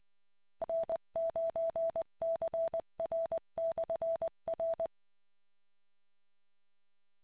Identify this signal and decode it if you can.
Morse code, heard as a keyed tone, about 20 wpm, 660 Hz. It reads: N9CR/R